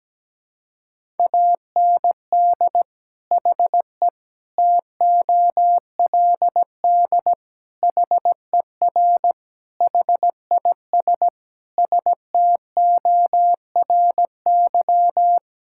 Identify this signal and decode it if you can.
Morse code, heard as a keyed tone, about 17 wpm, 700 Hz. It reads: AND HE TOLD HER HIS STORY
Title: Sentences from Top 1000 Words - Review Part 5 - 17wpm